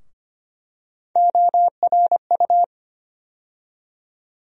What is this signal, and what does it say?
Morse code, heard as a keyed tone, about 25 wpm, 700 Hz. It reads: ORU
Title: USA Call Sign Postfixes - 3 Letters 25wpm